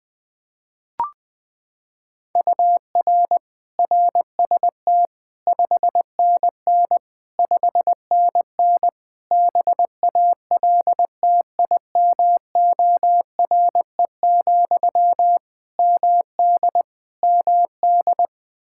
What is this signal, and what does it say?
Morse code, heard as a keyed tone, about 20 wpm, 700 Hz. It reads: UR RST 5NN 5NN BALTIMORE, MD MD
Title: Practice QSO - Exchange 1 - Beginning - 20wpm